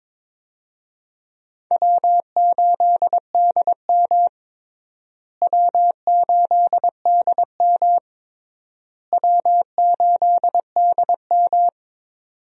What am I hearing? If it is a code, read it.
Morse code, heard as a keyed tone, about 22 wpm, 700 Hz. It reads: W8DM W8DM W8DM
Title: USA Call Signs - 4 Characters - Repeated 3x - 22wpm